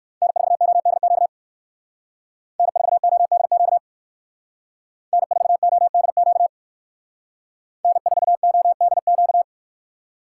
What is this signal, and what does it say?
Morse code, heard as a keyed tone, about 62 wpm, 700 Hz. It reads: N4ODQ N4ODQ N4ODQ N4ODQ